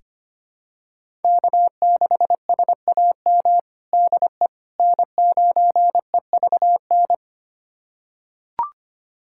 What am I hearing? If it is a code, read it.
Morse code, heard as a keyed tone, about 25 wpm, 700 Hz. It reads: K6SAM DE N9EVN
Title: USA Callsign DE Callsign - 5 Characters - 25wpm